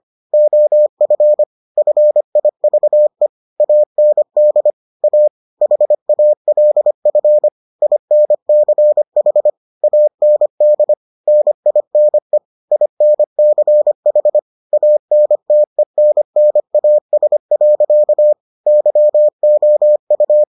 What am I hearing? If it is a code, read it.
Morse code, heard as a keyed tone, about 25 wpm, 600 Hz. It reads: OF FIVE AND A HALF INCH AND NINE INCH ANTENNAS. YOU